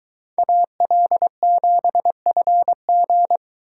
Morse code, heard as a keyed tone, about 22 wpm, 700 Hz. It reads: AL7FG